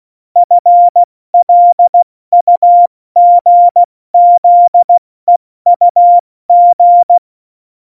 Morse code, heard as a keyed tone, about 16 wpm, 700 Hz. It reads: FLUGZEUG
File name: Morsekurs / Witze / Witz 19. Januar 2021, 70 F80